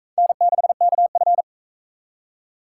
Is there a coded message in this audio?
Yes: Morse code from 0.2 to 1.4 s, heard as a keyed tone, about 41 wpm, 700 Hz. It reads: N6KF